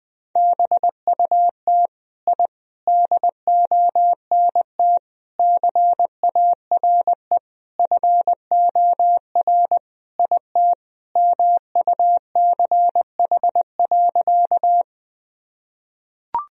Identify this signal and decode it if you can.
Morse code, heard as a keyed tone, about 21 wpm, 700 Hz. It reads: BUT I DONT CARE FOR IT MUCH.
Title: Sentences from Top 1000 Words - Review Part 1 - 20wpm